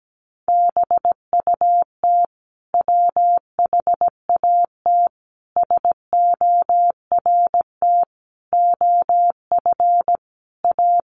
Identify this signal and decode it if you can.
Morse code, heard as a keyed tone, about 17 wpm, 700 Hz. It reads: BUT WHAT SORT OF A
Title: Sentences from Top 1300 Words 17wpm